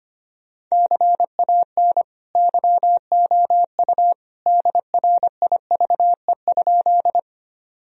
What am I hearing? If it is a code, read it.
Morse code, heard as a keyed tone, about 25 wpm, 700 Hz. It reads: CAN YOU DRIVE?
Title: Sentences from Top 1000 Words - Review Part 2 - 25wpm